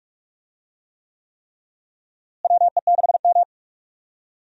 Morse code, heard as a keyed tone, about 46 wpm, 700 Hz. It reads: WE6M
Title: USA Call Signs - 4 Characters - Repeated 3x - 45wpm